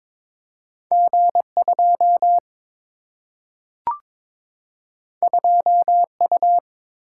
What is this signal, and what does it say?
Morse code, heard as a keyed tone, about 22 wpm, 700 Hz. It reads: G 2 2 U
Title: Single-Letter Single-Number 22wpm